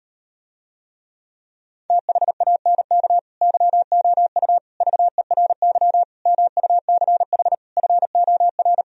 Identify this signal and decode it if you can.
Morse code, heard as a keyed tone, about 38 wpm, 700 Hz. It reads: THANK YOU VERY MUCH FOR